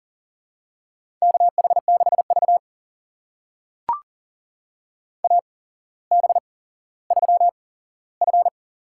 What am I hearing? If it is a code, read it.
Morse code, heard as a keyed tone, about 39 wpm, spaced as for 17 wpm, 700 Hz. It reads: KH6V AB3F
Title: USA Call Signs - 4 Characters - 8x Character Spacing - 40wpm